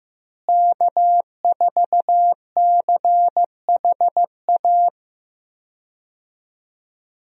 Morse code, heard as a keyed tone, about 15 wpm, 700 Hz. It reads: K4CHA
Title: USA Call Signs - 5 Characters - Repeated 2x - 15wpm